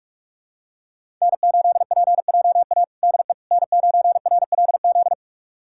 Morse code, heard as a keyed tone, about 41 wpm, 700 Hz. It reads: N8PJA DE N9RLZ
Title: USA Callsign DE Callsign - 5 Characters - 45wpm